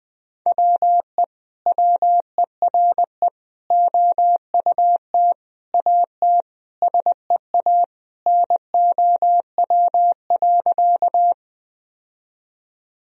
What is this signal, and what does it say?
Morse code, heard as a keyed tone, about 21 wpm, 700 Hz. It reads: WE WERE OUT AT SEA NOW.
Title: Sentences from Top 1000 Words - Review Part 1 - 20wpm